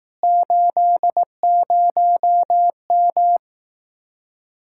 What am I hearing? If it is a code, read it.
Morse code, heard as a keyed tone, about 18 wpm, 700 Hz. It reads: 80M
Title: Speed-Racing Common QSO Phrases Encore - 15wpm